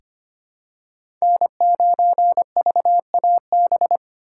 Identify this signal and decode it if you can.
Morse code, heard as a keyed tone, about 25 wpm, 700 Hz. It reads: N9VAB
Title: USA Call Signs - 4x Character Spacing - 25wpm